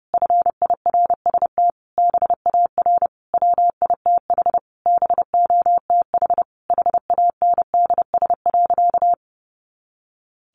Morse code, heard as a keyed tone, about 30 wpm, 700 Hz. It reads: FIRST BAR WITH BOTH HANDS.
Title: Easy Phrases 30wpm